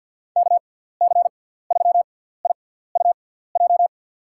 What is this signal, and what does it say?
Morse code, heard as a keyed tone, about 54 wpm, spaced as for 17 wpm, 700 Hz. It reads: KC3IUJ
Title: USA Call Signs - 6 Characters - 6x Character Spacing - 50wpm